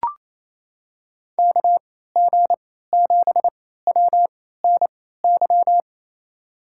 Morse code, heard as a keyed tone, about 28 wpm, spaced as for 16 wpm, 700 Hz. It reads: KG7WNY
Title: USA Call Signs - 6 Characters - 3x Character Spacing - 28wpm